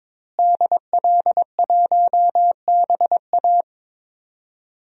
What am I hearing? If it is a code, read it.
Morse code, heard as a keyed tone, about 22 wpm, 700 Hz. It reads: DL1BA
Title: DXCC-100 Call Signs - 5 Characters - Repeated 2x - 22wpm